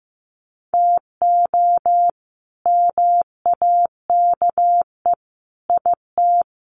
Morse code, heard as a keyed tone, about 15 wpm, 700 Hz. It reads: TO MAKE IT
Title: Sets of 3 Words 15wpm